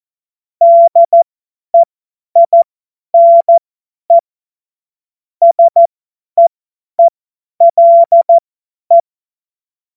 Morse code, heard as a keyed tone, about 14 wpm, spaced as for 10 wpm, 680 Hz. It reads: DEINE SEELE
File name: Morsekurs / Witze / Witz 27 Februar 2024 50 F70